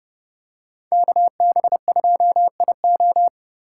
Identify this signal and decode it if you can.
Morse code, heard as a keyed tone, about 30 wpm, 700 Hz. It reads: KB2IO
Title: USA Call Signs - 5 Characters - 8x Character Spacing - 30wpm